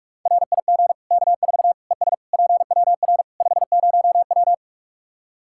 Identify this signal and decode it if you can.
Morse code, heard as a keyed tone, about 45 wpm, 700 Hz. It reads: RIG K4 ES PWR 50W
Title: Practice QSO - Exchange 2 - Info - 45wpm